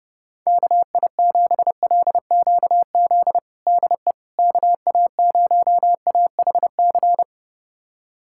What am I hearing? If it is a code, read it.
Morse code, heard as a keyed tone, about 32 wpm, 700 Hz. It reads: KI7LQZ DE KA0AHC
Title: USA Callsign DE Callsign 30wpm